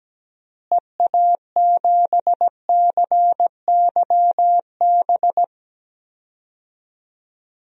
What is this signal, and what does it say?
Morse code, heard as a keyed tone, about 17 wpm, 700 Hz. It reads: EA7CYB
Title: DXCC-100 Call Signs - 6 Characters - Repeated 3x - 17wpm